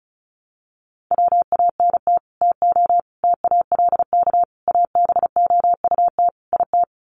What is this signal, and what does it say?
Morse code, heard as a keyed tone, about 34 wpm, 700 Hz. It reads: WANT TO TALK ABOUT IT
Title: Sets of 5 Words Encore 35wpm